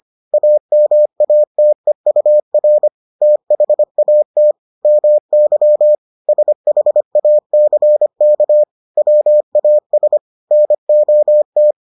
Morse code, heard as a keyed tone, about 25 wpm, 600 Hz. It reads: AMATEUR THAT MY SHACK WAS NOT